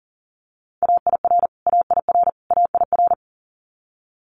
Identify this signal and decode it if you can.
Morse code, heard as a keyed tone, about 40 wpm, 700 Hz. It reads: AIR AIR AIR
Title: Multi-letter Phonemes Repeated 3x 40wpm